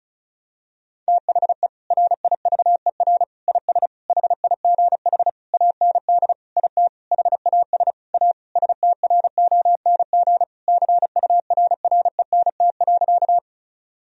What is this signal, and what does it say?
Morse code, heard as a keyed tone, about 36 wpm, 700 Hz. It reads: THE RIVER IS HIGH AND IT HAS A STRONG CURRENT.
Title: Sentences from Top 1000 Words - Review Part 4 - 35wpm